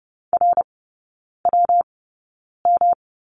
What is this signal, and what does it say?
Morse code, heard as a keyed tone, about 30 wpm, spaced as for 9 wpm, 700 Hz. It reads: RWM
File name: Lesson 19e - Letter w - Three Characters 30wpm